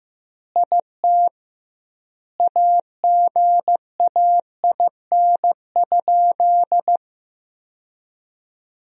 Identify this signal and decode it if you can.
Morse code, heard as a keyed tone, about 15 wpm, 700 Hz. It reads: IT AGAIN?
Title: 2x Word-Spacing - Sentences from Top 200 Words - 15wpm